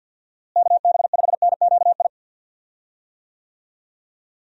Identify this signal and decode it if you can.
Morse code, heard as a keyed tone, about 51 wpm, 700 Hz. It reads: KB5NQI
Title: USA Call Signs - 6 Characters - Repeated 2x - 50wpm